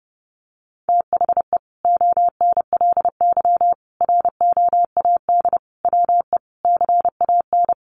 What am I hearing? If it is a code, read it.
Morse code, heard as a keyed tone, about 30 wpm, 700 Hz. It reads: THE ONLY ROAD WE CAN
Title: Sentences from Top 1100 Words 30wpm